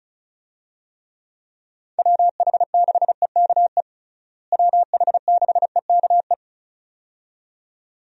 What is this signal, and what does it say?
Morse code, heard as a keyed tone, about 36 wpm, 700 Hz. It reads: WH6EKE WH6EKE
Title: USA Call Signs - 6 Characters - Repeated 2x - 35wpm